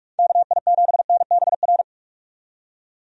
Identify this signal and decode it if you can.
Morse code, heard as a keyed tone, about 46 wpm, 700 Hz. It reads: KI7NBR
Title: USA Call Signs - 6 Characters - 45wpm